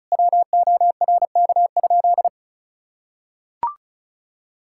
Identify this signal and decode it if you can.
Morse code, heard as a keyed tone, about 36 wpm, 700 Hz. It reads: WORK?
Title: Sentences from Top 1000 Words - Review Part 6 - 35wpm